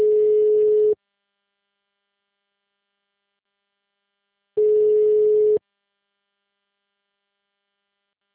Ringback tone (Europe, 425 Hz).